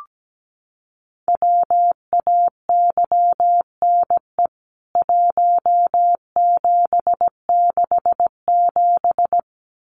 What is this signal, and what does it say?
Morse code, heard as a keyed tone, about 17 wpm, 700 Hz. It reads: WAYNE 1767
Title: CWT Exchange 17wpm